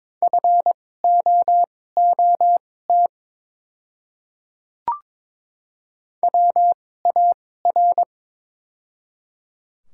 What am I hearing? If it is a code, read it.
Morse code, heard as a keyed tone, about 22 wpm, spaced as for 16 wpm, 690 Hz. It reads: FOOT WAR
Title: 2x Character Spacing - Top 500 Words - 22wpm